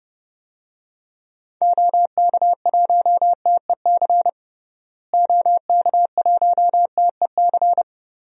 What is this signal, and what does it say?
Morse code, heard as a keyed tone, about 30 wpm, 700 Hz. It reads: OK1TEC OK1TEC
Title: DXCC-100 Call Signs - 6 Characters - Repeated 3x - 30wpm